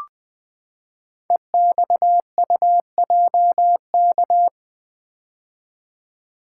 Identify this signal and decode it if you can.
Morse code, heard as a keyed tone, about 20 wpm, 700 Hz. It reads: EXUJK
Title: Code Groups - 5 Letters - 20wpm